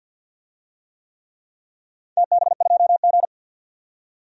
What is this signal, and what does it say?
Morse code, heard as a keyed tone, about 47 wpm, 700 Hz. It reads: TB2G
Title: DXCC-100 Call Signs - 4 Characters - 50wpm